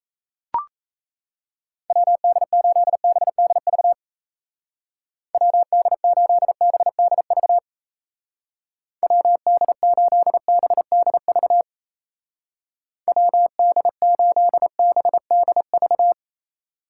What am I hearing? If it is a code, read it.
Morse code, heard as a keyed tone, about 34 wpm, 700 Hz. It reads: WD8BDV WD8BDV WD8BDV WD8BDV
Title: Speed Racing - USA 2x3 Call Signs 28wpm